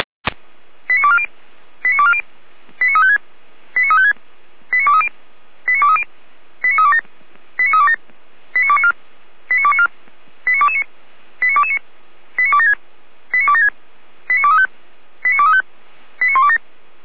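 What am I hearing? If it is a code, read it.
Five-tone selective call (ZVEI), heard as quick runs of five tones, tones of about 70 ms. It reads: 89240 89240 89356 89356 89230 89230 89327 89327 89294 89294 89229 89229 89276 89276 89235 89235 89127
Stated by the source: ZVEI five-tone selective call received on an SDR.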